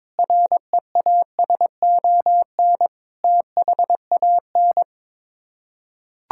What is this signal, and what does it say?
Morse code, heard as a keyed tone, about 22 wpm, 700 Hz. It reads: REASON THAN
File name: practice / 22wpm / Sets of 6 Words - 22wpm